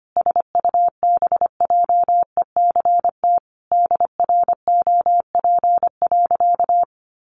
Morse code, heard as a keyed tone, about 25 wpm, 700 Hz. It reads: SUBJECT DROP.
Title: Sentences from Top 1400 Words 25wpm